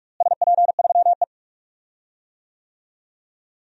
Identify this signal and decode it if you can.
Morse code, heard as a keyed tone, about 46 wpm, 700 Hz. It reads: SP3E
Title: DXCC-100 Call Signs - 4 Characters - Repeated 3x - 45wpm